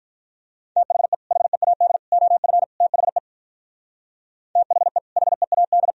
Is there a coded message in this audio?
Yes: Morse code from 0.8 to 5.9 s, heard as a keyed tone, about 44 wpm, 700 Hz. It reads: THE HEAD OF THE THE HEAD